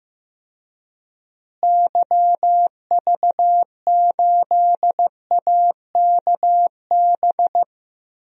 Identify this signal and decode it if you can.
Morse code, heard as a keyed tone, about 15 wpm, 700 Hz. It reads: YV8AKB
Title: DXCC-100 Call Signs - 6 Characters - Repeated 3x - 15wpm